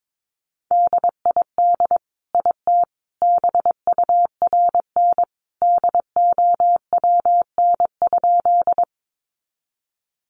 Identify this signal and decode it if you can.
Morse code, heard as a keyed tone, about 22 wpm, 700 Hz. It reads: DID IT BURN DOWN?
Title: Sentences from Top 1300 Words 22wpm